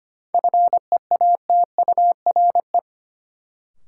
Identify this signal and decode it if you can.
Morse code, heard as a keyed tone, about 25 wpm, 700 Hz. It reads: FEATURE
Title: Sentences from Top 800 Words - New Word Drill - 25wpm